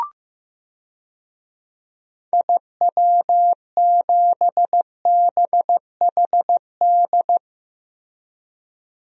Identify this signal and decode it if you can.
Morse code, heard as a keyed tone, about 15 wpm, 700 Hz. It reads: IW7BHD
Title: DXCC-100 Call Signs - 6 Characters - Repeated 3x - 15wpm